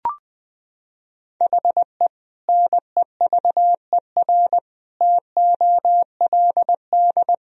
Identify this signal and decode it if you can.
Morse code, heard as a keyed tone, about 20 wpm, 700 Hz. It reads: HE NEVER TOLD
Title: Sentences from Top 400 Words and Abbreviations 20wpm